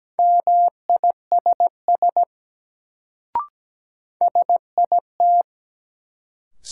Morse code, heard as a keyed tone, about 17 wpm, 700 Hz. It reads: MISS SIT
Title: Sentences from Top 500 Words - New Word Drill - 17wpm